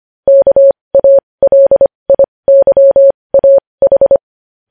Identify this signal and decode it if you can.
Morse code, heard as a keyed tone, about 25 wpm, spaced as for 20 wpm, 550 Hz. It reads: KALIYAH
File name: podcast / News headlines-1x-25x20